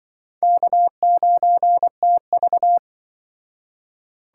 Morse code, heard as a keyed tone, about 24 wpm, 700 Hz. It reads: K9TV
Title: Speed Racing - USA 1x2 Call Signs 20wpm